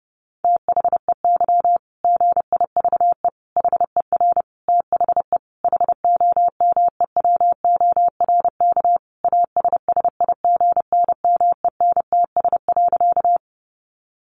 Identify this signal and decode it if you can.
Morse code, heard as a keyed tone, about 30 wpm, 700 Hz. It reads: THEY GIVE HER THE HOMEWORK ASSIGNMENTS.
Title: Easy Phrases 30wpm